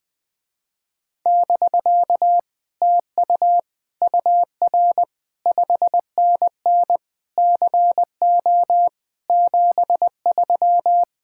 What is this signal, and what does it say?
Morse code, heard as a keyed tone, about 20 wpm, 700 Hz. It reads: <BK> TU UR 5NN CO 73
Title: Speed-Racing Common QSO Phrases - 20wpm